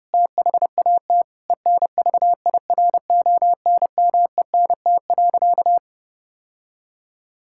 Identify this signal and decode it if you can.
Morse code, heard as a keyed tone, about 30 wpm, 700 Hz. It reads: THAT ENVIRONMENT.
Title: Sentences from Top 1000 Words - Review Part 5 - 30wpm